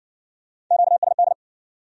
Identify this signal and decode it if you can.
Morse code, heard as a keyed tone, about 54 wpm, 700 Hz. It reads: QSD